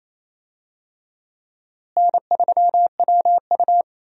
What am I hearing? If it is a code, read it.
Morse code, heard as a keyed tone, about 28 wpm, 700 Hz. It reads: N3WU
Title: USA Call Signs - 4 Characters - Repeated 2x - 28wpm